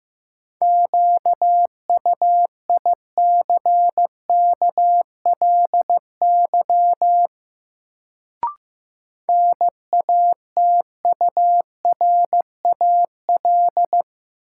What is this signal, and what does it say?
Morse code, heard as a keyed tone, about 15 wpm, 700 Hz. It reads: QUICKLY NATURAL
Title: Sentences from Top 700 Words - New Word Drill - 15wpm